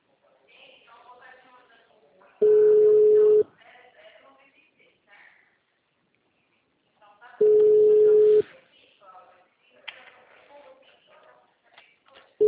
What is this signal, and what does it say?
Ringback tone (Europe, 425 Hz), with speech.
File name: out- -2021- - - .3402